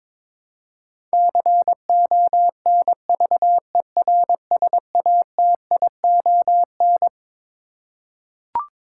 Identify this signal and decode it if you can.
Morse code, heard as a keyed tone, about 21 wpm, 700 Hz. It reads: CONVERSATION
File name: Twelve Letter Words - 22wpm